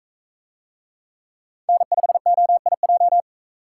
Morse code, heard as a keyed tone, about 44 wpm, 700 Hz. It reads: N5OIJ